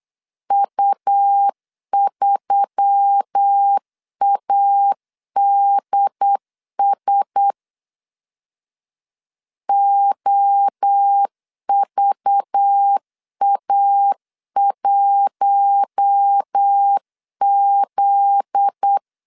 Morse code, heard as a keyed tone, about 8 wpm, 800 Hz. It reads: U3ADS OVA1Z
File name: LEZIONI / 8 YJ45 / 8 YJ 8wpm 3313